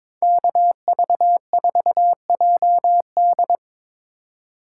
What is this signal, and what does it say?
Morse code, heard as a keyed tone, about 26 wpm, 700 Hz. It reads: KV4JD
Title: Speed Racing - USA 2x2 Call Signs 22wpm